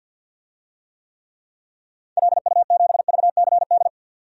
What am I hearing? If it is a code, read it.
Morse code, heard as a keyed tone, about 47 wpm, 700 Hz. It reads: LU7VCD
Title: DXCC-100 Call Signs - 6 Characters - 50wpm